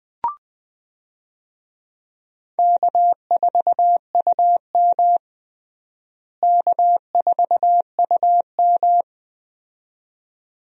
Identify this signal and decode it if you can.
Morse code, heard as a keyed tone, about 21 wpm, 700 Hz. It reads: K4UM K4UM
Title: USA Call Signs - 4 Characters - Repeated 2x - 20wpm